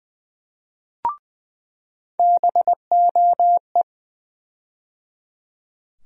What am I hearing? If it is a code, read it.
Morse code, heard as a keyed tone, about 21 wpm, 700 Hz. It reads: BOE
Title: USA Call Sign Postfixes - 3 Letters 20wpm